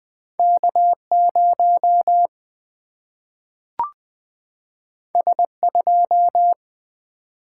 Morse code, heard as a keyed tone, about 20 wpm, 700 Hz. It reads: K0 S2
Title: Single-Letter Single-Number 20wpm